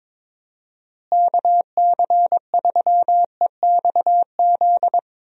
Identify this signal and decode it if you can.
Morse code, heard as a keyed tone, about 21 wpm, 700 Hz. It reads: KC3EXZ
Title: USA Call Signs - 6 Characters - Repeated 3x - 22wpm